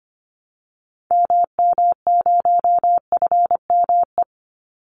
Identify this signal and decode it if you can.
Morse code, heard as a keyed tone, about 25 wpm, 700 Hz. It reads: MM0FME
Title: Call Signs - 25wpm